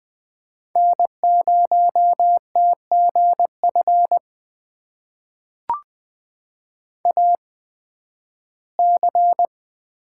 Morse code, heard as a keyed tone, about 20 wpm, spaced as for 13 wpm, 700 Hz. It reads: N0TGF AC
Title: USA Call Signs - 5 Characters - 8x Character Spacing - 20wpm